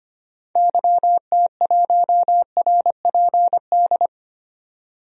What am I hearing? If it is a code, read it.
Morse code, heard as a keyed tone, about 25 wpm, 700 Hz. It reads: YT1RPD
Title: DXCC-100 Call Signs - 6 Characters - 25wpm